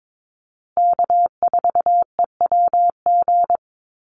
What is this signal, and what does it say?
Morse code, heard as a keyed tone, about 22 wpm, 700 Hz. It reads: K4EWG
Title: Call Signs Repeated 2x - 22wpm